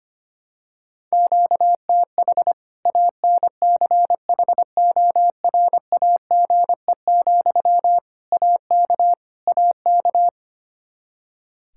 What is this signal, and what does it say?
Morse code, heard as a keyed tone, about 25 wpm, 700 Hz. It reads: QTH ANCHORAGE, AK AK
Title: Practice QSO - Exchange 1 - Beginning - 25wpm